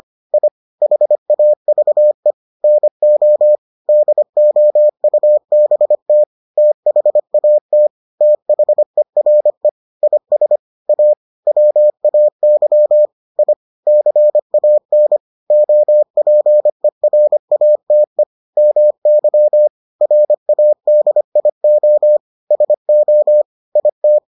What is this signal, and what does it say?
Morse code, heard as a keyed tone, about 25 wpm, 600 Hz. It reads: I HAVE NO DOUBT THAT THERE IS A WAY I CAN OPERATE MY RADIO SO IT